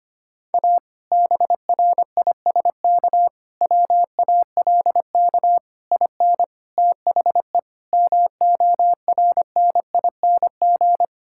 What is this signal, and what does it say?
Morse code, heard as a keyed tone, about 25 wpm, 700 Hz. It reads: A BRISK WALK IN THE MORNING
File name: practice / 25wpm / Difficult Sentences with Hints - 25wpm